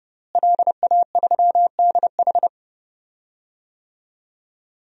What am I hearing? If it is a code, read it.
Morse code, heard as a keyed tone, about 32 wpm, 700 Hz. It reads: LA3DH
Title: DXCC-100 Call Signs - 5 Characters - Repeated 2x - 30wpm